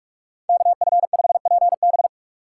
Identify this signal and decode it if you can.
Morse code, heard as a keyed tone, about 44 wpm, 700 Hz. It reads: KF5PB